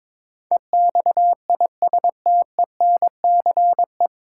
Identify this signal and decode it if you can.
Morse code, heard as a keyed tone, about 22 wpm, 700 Hz. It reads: EXISTENCE
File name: 9 Letter Words - First 3 Letters Given - 22wpm